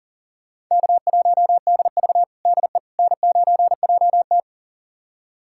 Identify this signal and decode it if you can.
Morse code, heard as a keyed tone, about 40 wpm, 700 Hz. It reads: K1DV DE N9JT
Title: USA Callsign DE Callsign - 4 Characters - 40wpm